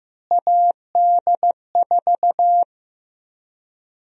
Morse code, heard as a keyed tone, about 15 wpm, 700 Hz. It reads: AD4
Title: USA Call Sign Prefixes - Repeated 2x - 15wpm